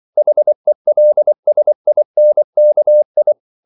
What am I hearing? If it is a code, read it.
Morse code, heard as a keyed tone, about 24 wpm, 600 Hz. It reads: HELSINKI
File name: CW / Practice / NationalCapitals24